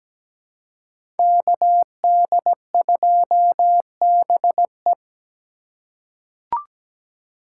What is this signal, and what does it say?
Morse code, heard as a keyed tone, about 17 wpm, 700 Hz. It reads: KD2BE
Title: USA Call Signs - 5 Characters - 2x Character Spacing - 17wpm